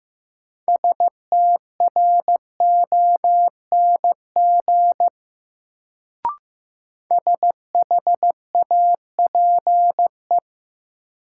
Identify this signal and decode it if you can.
Morse code, heard as a keyed tone, about 15 wpm, 700 Hz. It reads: STRONG SHAPE
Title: Sentences from Top 500 Words - New Word Drill - 15wpm